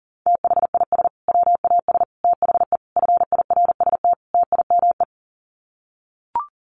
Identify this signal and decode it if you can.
Morse code, heard as a keyed tone, about 40 wpm, 700 Hz. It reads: THIS WAS THE FIRST TIME
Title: Sets of 5 Words Encore 40wpm